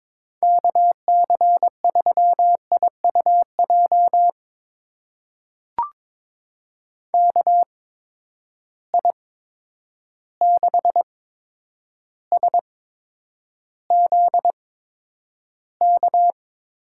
Morse code, heard as a keyed tone, about 22 wpm, spaced as for 9 wpm, 700 Hz. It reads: KC3IUJ KI6SZK
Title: USA Call Signs - 6 Characters - 8x Character Spacing - 22wpm